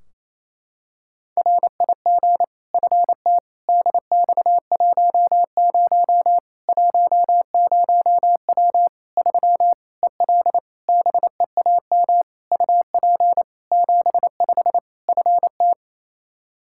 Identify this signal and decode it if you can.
Morse code, heard as a keyed tone, about 28 wpm, 700 Hz. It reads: RIG FT DX10 10W 3 EL BEAM UP 75 FT
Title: K1USN SST QSO Practice 28wpm